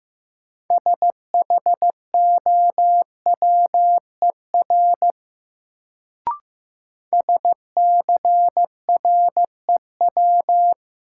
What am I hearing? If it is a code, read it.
Morse code, heard as a keyed tone, about 15 wpm, 700 Hz. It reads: SHOWER SCREW
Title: Sentences from Top 1600 Words - New Word Drill - 15wpm